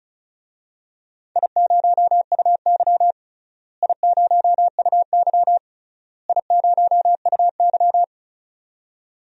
Morse code, heard as a keyed tone, about 34 wpm, 700 Hz. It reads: I0UY I0UY I0UY
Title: DXCC-100 Call Signs - 4 Characters - Repeated 3x - 35wpm